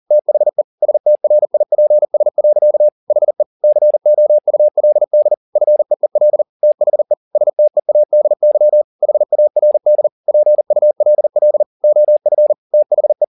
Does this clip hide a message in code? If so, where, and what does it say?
Yes: Morse code from 0.1 to 13.2 s, heard as a keyed tone, about 40 wpm, 600 Hz. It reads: THE STRIPS. HE COULD FEEL THE STEADY HARD PULL OF THE